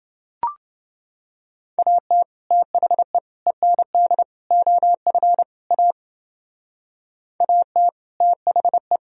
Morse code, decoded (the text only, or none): AT THE END OF A AT THE